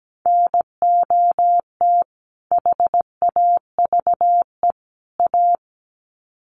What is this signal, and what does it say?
Morse code, heard as a keyed tone, about 17 wpm, 700 Hz. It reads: NOT HAVE A
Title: Sets of 3 Words 17wpm